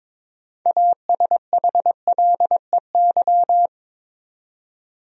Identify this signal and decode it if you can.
Morse code, heard as a keyed tone, about 22 wpm, 700 Hz. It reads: ASHLEY
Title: USA Common Names - 22wpm